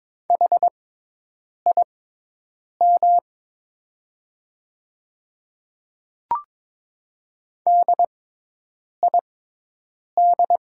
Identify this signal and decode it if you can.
Morse code, heard as a keyed tone, about 22 wpm, spaced as for 8 wpm, 700 Hz. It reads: HIM DID
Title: 6x Character Spacing - Top 100 Words - 22wpm